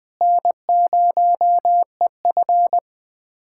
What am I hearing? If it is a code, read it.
Morse code, heard as a keyed tone, about 21 wpm, 700 Hz. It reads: N0EF
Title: USA Callsign DE Callsign - 4 Characters - 20wpm